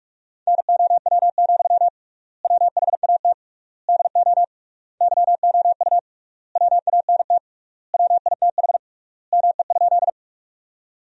Morse code, heard as a keyed tone, about 47 wpm, 700 Hz. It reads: NOW, WHAT DO YOU WANT WITH ME?
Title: 3x Word-Spacing - Sentences from Top 200 Words - 45wpm